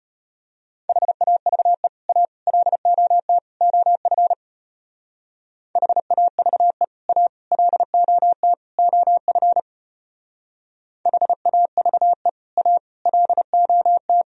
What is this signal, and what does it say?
Morse code, heard as a keyed tone, about 31 wpm, 700 Hz. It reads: HAVE A LOT OF HAVE A LOT OF HAVE A LOT